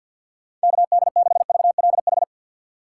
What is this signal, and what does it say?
Morse code, heard as a keyed tone, about 46 wpm, 700 Hz. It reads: KD6VLH
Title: USA Call Signs - 2x Character Spacing - 50wpm